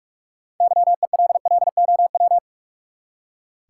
Morse code, heard as a keyed tone, about 45 wpm, 700 Hz. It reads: YELLOW
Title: Six Letter Words - 45wpm